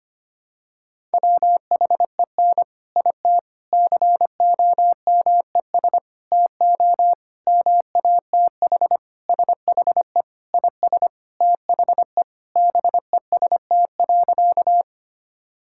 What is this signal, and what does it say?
Morse code, heard as a keyed tone, about 25 wpm, 700 Hz. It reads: WHEN IT COMES TO MATH SHE IS THE BEST.
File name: practice / 25wpm / Sentences with 3-Word Phrases - 25wpm